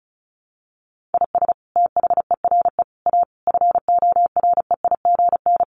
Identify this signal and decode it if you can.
Morse code, heard as a keyed tone, about 34 wpm, 700 Hz. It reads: IS THERE A FOREIGN
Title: Sentences from Top 1400 Words 35wpm